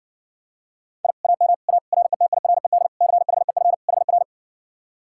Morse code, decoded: I AM A RETIRED CHEF HR

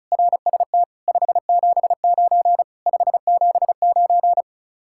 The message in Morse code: RST 579 579

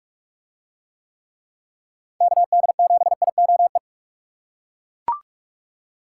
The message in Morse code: KD7IOE

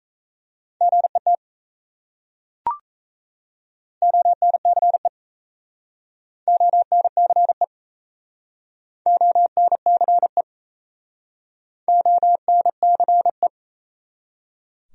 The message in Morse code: GET ONCE ONCE ONCE ONCE

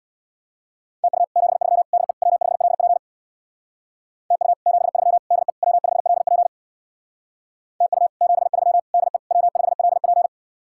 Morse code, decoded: TU 73 DE W5LP TU 73 DE W5LP TU 73 DE W5LP

